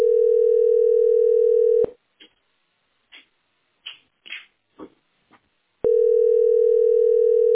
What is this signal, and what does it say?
Ringback tone (North America), 440 + 480 Hz.